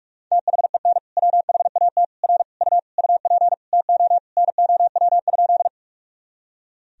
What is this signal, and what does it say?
Morse code, heard as a keyed tone, about 46 wpm, 700 Hz. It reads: THEN WHAT R U UP TO NOW?
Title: Sentences from Top 100 Words and Abbreviations 45wpm